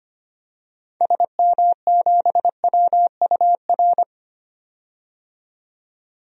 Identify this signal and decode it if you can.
Morse code, heard as a keyed tone, about 25 wpm, 700 Hz. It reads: SM7WUR
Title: DXCC-100 Call Signs - 6 Characters - 25wpm